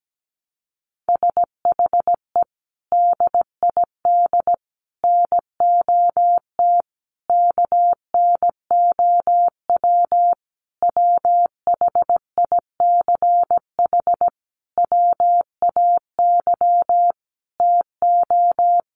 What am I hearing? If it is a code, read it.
Morse code, heard as a keyed tone, about 17 wpm, 700 Hz. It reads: SHE DID NOT KNOW WHICH WAY TO
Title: Sentences from Top 1100 Words 17wpm